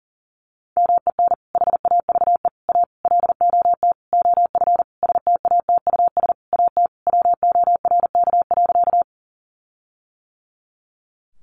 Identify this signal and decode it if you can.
Morse code, heard as a keyed tone, about 40 wpm, 700 Hz. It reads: MEN HAVE A LOT OF STATUS AT WORK.
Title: Sentences from Top 1400 Words 40wpm